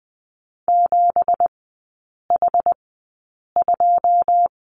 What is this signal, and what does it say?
Morse code, heard as a keyed tone, about 20 wpm, spaced as for 8 wpm, 700 Hz. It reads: 7H2
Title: 205 Lesson 33e - Letter j - Three Characters 20wpm